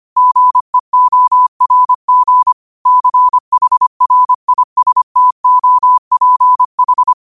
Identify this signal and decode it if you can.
Morse code, heard as a keyed tone, about 25 wpm, 1000 Hz. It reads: GEORG CHRISTOPH
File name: qotdfiles / 25wpm